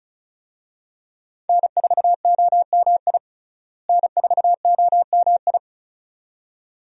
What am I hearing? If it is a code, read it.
Morse code, heard as a keyed tone, about 34 wpm, 700 Hz. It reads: N4OMI N4OMI
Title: USA Call Signs - 5 Characters - Repeated 2x - 35wpm